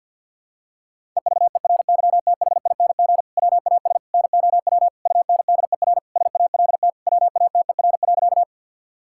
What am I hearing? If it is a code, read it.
Morse code, heard as a keyed tone, about 48 wpm, 700 Hz. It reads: EVERYTHING WAS NOW UNDER SALT WATER.
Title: Sentences from Top 1000 Words - Review Part 5 - 50wpm